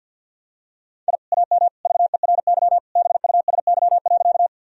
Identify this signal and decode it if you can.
Morse code, heard as a keyed tone, about 50 wpm, 700 Hz. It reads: I AM VERY BUSY.
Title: Sentences from Top 1000 Words - Review Part 3 - 50wpm